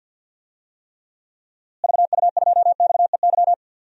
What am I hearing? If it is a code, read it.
Morse code, heard as a keyed tone, about 47 wpm, 700 Hz. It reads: VU2XEY
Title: DXCC-100 Call Signs - 6 Characters - Repeated 2x - 50wpm